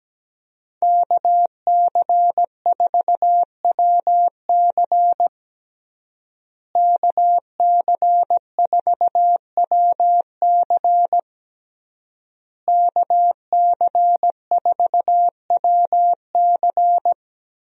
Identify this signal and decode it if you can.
Morse code, heard as a keyed tone, about 17 wpm, 700 Hz. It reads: KC4WC KC4WC KC4WC
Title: USA Call Signs - 5 Characters - Repeated 3x - 17wpm